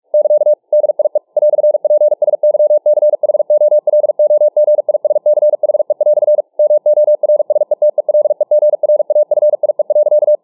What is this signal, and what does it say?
Morse code, heard as a keyed tone, about 44 wpm, 600 Hz. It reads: <KA> DIE "PSYCHOLOGISCHE" MORSETELEGRAFIE.